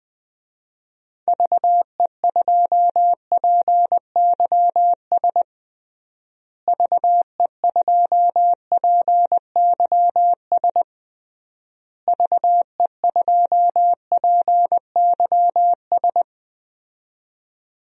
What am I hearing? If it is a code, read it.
Morse code, heard as a keyed tone, about 21 wpm, 700 Hz. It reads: VE2PYS VE2PYS VE2PYS
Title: DXCC-100 Call Signs - 6 Characters - Repeated 3x - 20wpm